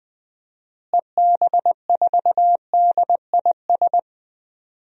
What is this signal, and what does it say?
Morse code, heard as a keyed tone, about 20 wpm, 700 Hz. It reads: EB4DIS
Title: DXCC-100 Call Signs - 6 Characters - Repeated 3x - 20wpm